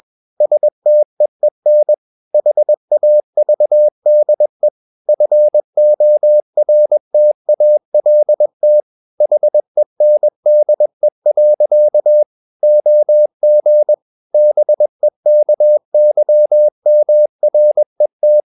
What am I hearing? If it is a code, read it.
Morse code, heard as a keyed tone, about 21 wpm, 600 Hz. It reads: STEEN HAVDE FORTALT HENDE. OG BEKYMRET
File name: Ferie QRS 21wpm